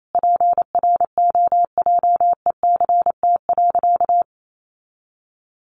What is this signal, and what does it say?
Morse code, heard as a keyed tone, about 28 wpm, 700 Hz. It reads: PROJECT.
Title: Sentences from Top 1400 Words 28wpm